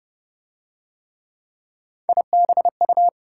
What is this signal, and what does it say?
Morse code, heard as a keyed tone, about 29 wpm, 700 Hz. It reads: IBU